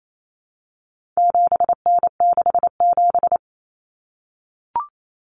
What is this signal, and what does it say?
Morse code, heard as a keyed tone, about 28 wpm, 700 Hz. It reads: 7N67
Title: Common Cut Numbers - 4-Digit 28wpm